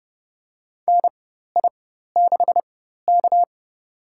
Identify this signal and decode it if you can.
Morse code, heard as a keyed tone, about 30 wpm, spaced as for 14 wpm, 700 Hz. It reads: NI6K
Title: USA Call Signs - 4x Character Spacing - 30wpm